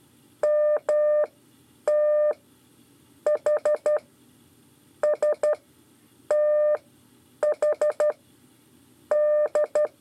Morse code, heard as a keyed tone, about 12 wpm, spaced as for 8 wpm, 600 Hz. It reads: MT H STH D
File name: 5 LEZIONE RIPASSO 1